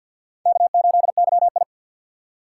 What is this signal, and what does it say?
Morse code, heard as a keyed tone, about 53 wpm, 700 Hz. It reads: K8YI
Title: USA Call Signs - 4x Character Spacing - 50wpm